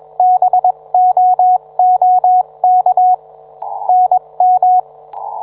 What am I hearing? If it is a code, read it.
Morse code, heard as a keyed tone, about 22 wpm, spaced as for 19 wpm, 730 Hz, with radio noise. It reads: BOOK NM